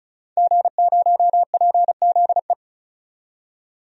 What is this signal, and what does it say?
Morse code, heard as a keyed tone, about 36 wpm, 700 Hz. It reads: G0PZE
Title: DXCC-100 Call Signs - 5 Characters - 35wpm